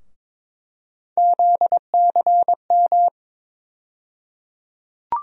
Morse code, heard as a keyed tone, about 22 wpm, 700 Hz. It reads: ZCM
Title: USA Call Sign Postfixes - 3 Letters 22wpm